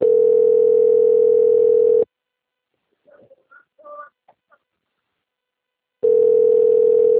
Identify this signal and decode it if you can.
Ringback tone (North America), 440 + 480 Hz.
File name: out- - - - - .798